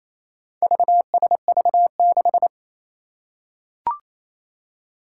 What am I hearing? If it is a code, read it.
Morse code, heard as a keyed tone, about 28 wpm, 700 Hz. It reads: VSV6
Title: Four Challenging Letter-Number Combinations 28wpm